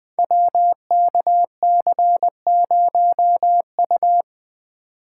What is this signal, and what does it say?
Morse code, heard as a keyed tone, about 20 wpm, 700 Hz. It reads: WKC0U
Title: Code Groups - 5 Letter:Numbers - 20wpm